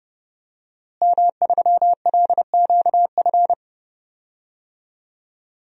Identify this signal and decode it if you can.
Morse code, heard as a keyed tone, about 30 wpm, 700 Hz. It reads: M3LQF
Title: DXCC-100 Call Signs - 5 Characters - Repeated 2x - 30wpm